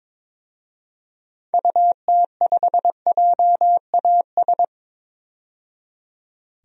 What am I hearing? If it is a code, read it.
Morse code, heard as a keyed tone, about 22 wpm, 700 Hz. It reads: UT5JAS
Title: DXCC-100 Call Signs - 6 Characters - 22wpm